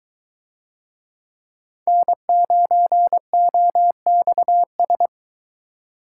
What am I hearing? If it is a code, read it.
Morse code, heard as a keyed tone, about 21 wpm, 700 Hz. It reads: N9OXS